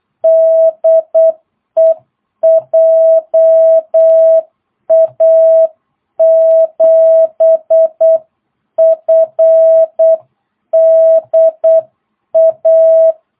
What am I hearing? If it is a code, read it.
Morse code, heard as a keyed tone, about 8 wpm, 650 Hz. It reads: DE JA7FDA
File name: jh8cbh / 2022cw